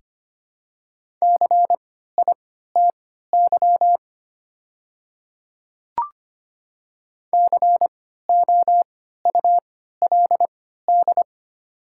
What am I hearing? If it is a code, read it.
Morse code, heard as a keyed tone, about 25 wpm, spaced as for 14 wpm, 700 Hz. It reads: CITY COULD
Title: 3x Character Spacing - Top 300 Words - 25wpm